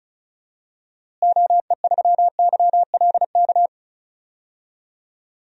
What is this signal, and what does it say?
Morse code, heard as a keyed tone, about 36 wpm, 700 Hz. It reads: OE3YLK
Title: DXCC-100 Call Signs - 6 Characters - Repeated 3x - 35wpm